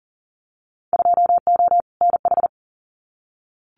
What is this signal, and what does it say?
Morse code, heard as a keyed tone, about 40 wpm, 700 Hz. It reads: 2O NH
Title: Winter Field Day Exchanges 40wpm